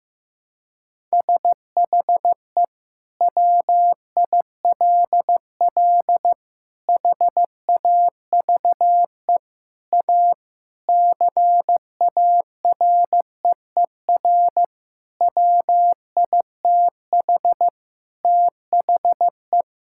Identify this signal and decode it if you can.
Morse code, heard as a keyed tone, about 15 wpm, 700 Hz. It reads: SHE WILL HAVE A CAREER WITH THE
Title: Sentences from Top 1000 Words - Review Part 6 - 15wpm